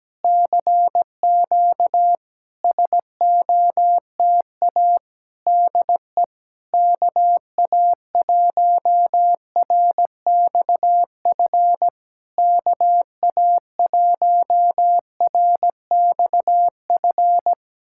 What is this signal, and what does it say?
Morse code, heard as a keyed tone, about 17 wpm, 690 Hz. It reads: CQ SOTA DE KA1RXF KA1RXF
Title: SOTA QSO Practice 17wpm